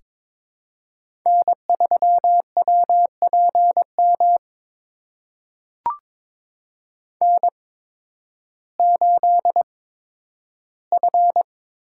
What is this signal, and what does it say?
Morse code, heard as a keyed tone, about 22 wpm, spaced as for 12 wpm, 700 Hz. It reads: N3WPM N8F
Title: USA Call Signs - 5 Characters - 8x Character Spacing - 22wpm